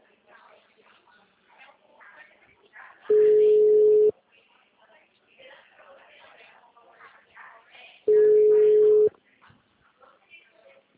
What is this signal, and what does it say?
Ringback tone (Europe, 425 Hz), with speech.